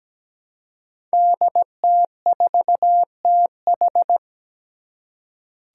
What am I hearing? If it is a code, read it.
Morse code, heard as a keyed tone, about 17 wpm, 700 Hz. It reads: DT4TH
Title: Five Challenging Letter-Number Combinations 17wpm